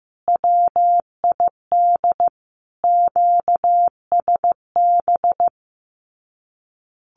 Morse code, heard as a keyed tone, about 15 wpm, 700 Hz. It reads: WID QSB
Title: Common QSO Phrases 15wpm